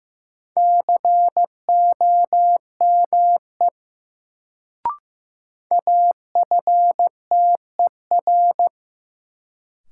Morse code, heard as a keyed tone, about 15 wpm, 700 Hz. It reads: COME AFTER
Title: Sets of 2 Words - New Word Drill - 15wpm